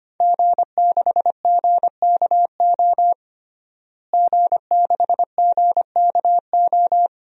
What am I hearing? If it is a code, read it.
Morse code, heard as a keyed tone, about 25 wpm, 700 Hz. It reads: G6GKO G6GKO
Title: DXCC-100 Call Signs - 5 Characters - Repeated 3x - 25wpm